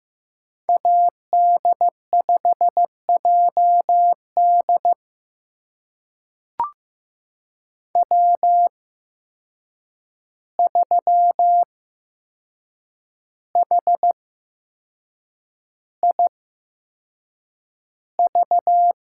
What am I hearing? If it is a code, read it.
Morse code, heard as a keyed tone, about 15 wpm, spaced as for 6 wpm, 700 Hz. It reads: AD5JD W3HIV
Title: USA Call Signs - 5 Characters - 8x Character Spacing - 15wpm